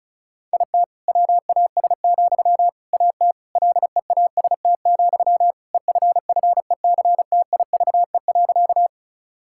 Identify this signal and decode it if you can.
Morse code, heard as a keyed tone, about 36 wpm, 700 Hz. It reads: IT WAS, AT LEAST, EFFECTIVE.
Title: Sentences from Top 1000 Words - Review Part 6 - 35wpm